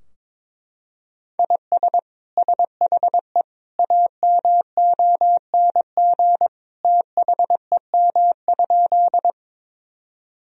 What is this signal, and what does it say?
Morse code, heard as a keyed tone, about 22 wpm, 700 Hz. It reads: IS SHE AMONG THEM?
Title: Sentences from Top 1000 Words - Review Part 3 - 22wpm